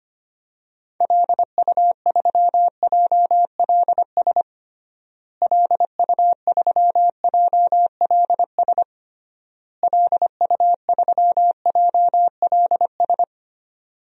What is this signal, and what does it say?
Morse code, heard as a keyed tone, about 25 wpm, 700 Hz. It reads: LU3JLS LU3JLS LU3JLS
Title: DXCC-100 Call Signs - 6 Characters - Repeated 3x - 25wpm